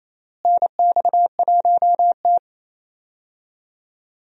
Morse code, heard as a keyed tone, about 28 wpm, 700 Hz. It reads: NX1T
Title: USA Call Signs - 4 Characters - Repeated 3x - 28wpm